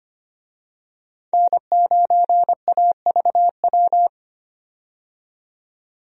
Morse code, heard as a keyed tone, about 25 wpm, 700 Hz. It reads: N9AVW
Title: USA Call Signs - 5 Characters - 25wpm